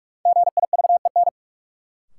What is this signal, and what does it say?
Morse code, heard as a keyed tone, about 41 wpm, 700 Hz. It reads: GIVEN
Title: Sentences from Top 1400 Words - New Word Drill Alt - 45wpm